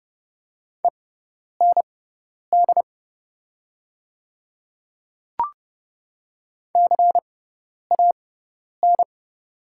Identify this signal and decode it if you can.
Morse code, heard as a keyed tone, about 30 wpm, spaced as for 10 wpm, 690 Hz. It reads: END CAN
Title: 6x Character Spacing - Top 300 Words - 30wpm